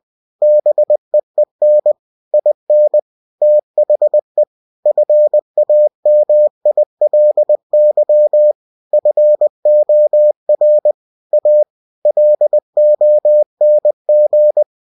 Morse code, heard as a keyed tone, about 20 wpm, 600 Hz. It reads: BEEN IN THE FAMILY FOR A LONG
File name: cw-books / winnie-the-pooh / winnie-20wpm-page-0020